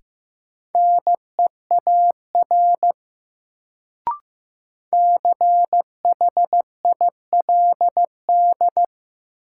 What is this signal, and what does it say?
Morse code, heard as a keyed tone, about 15 wpm, 700 Hz. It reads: NEAR CHILD
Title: Sets of 5 Words Encore - New Word Drill - 15wpm